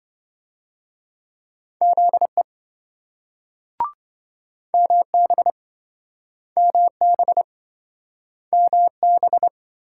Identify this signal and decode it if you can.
Morse code, heard as a keyed tone, about 26 wpm, 700 Hz. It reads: ZE MB MB MB